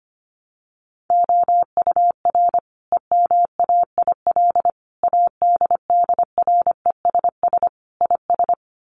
Morse code, heard as a keyed tone, about 25 wpm, 700 Hz. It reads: OUR EMAIL ADDRESS IS
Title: Sets of 5 Words Encore2 25wpm